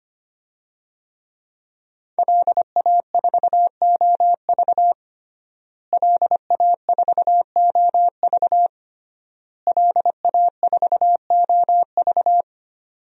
Morse code, heard as a keyed tone, about 25 wpm, 700 Hz. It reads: LA4OV LA4OV LA4OV
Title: DXCC-100 Call Signs - 5 Characters - Repeated 3x - 25wpm